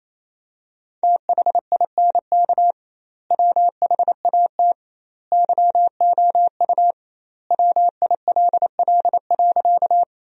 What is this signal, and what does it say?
Morse code, heard as a keyed tone, about 27 wpm, 700 Hz. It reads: THINK WHAT YOU WILL.
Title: 2x Word-Spacing - Sentences from Top 200 Words - 28wpm